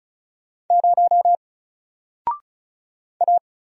Morse code, heard as a keyed tone, about 36 wpm, 700 Hz. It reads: 0 A
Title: Single Letter-Number Alt 35wpm